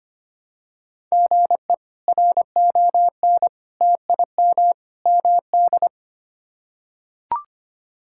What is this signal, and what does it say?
Morse code, heard as a keyed tone, about 25 wpm, 700 Hz. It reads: GE RON TIM MD
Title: K1USN SST QSO Practice - 25wpm